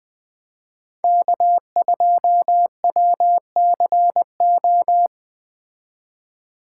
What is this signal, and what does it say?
Morse code, heard as a keyed tone, about 20 wpm, 700 Hz. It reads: K2WCO
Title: Speed Racing - USA 1x3 Call Signs 17wpm